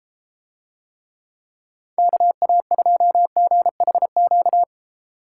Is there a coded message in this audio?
Yes: Morse code from 2.0 to 4.6 s, heard as a keyed tone, about 32 wpm, 700 Hz. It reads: KA2GHQ